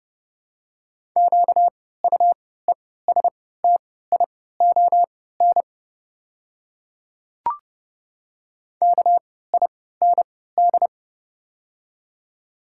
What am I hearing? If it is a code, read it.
Morse code, heard as a keyed tone, about 30 wpm, spaced as for 17 wpm, 700 Hz. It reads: QUESTION KIND
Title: 3x Character Spacing - Top 500 Words - 30wpm